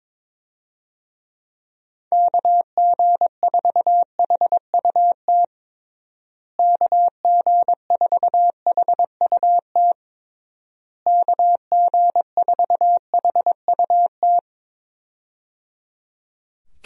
Morse code, heard as a keyed tone, about 22 wpm, 700 Hz. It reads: KG4HUT KG4HUT KG4HUT
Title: USA Call Signs - 6 Characters - Repeated 3x - 22wpm